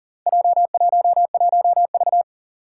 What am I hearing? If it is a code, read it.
Morse code, heard as a keyed tone, about 40 wpm, 700 Hz. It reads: J11V
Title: Four Challenging Letter-Number Combinations 40wpm